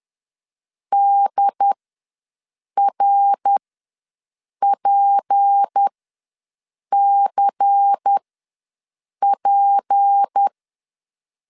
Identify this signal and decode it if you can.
Morse code, heard as a keyed tone, about 10 wpm, spaced as for 6 wpm, 800 Hz. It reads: DRPCP
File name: LEZIONI / 6 PKX0 / 6 P ws 10wpm 3751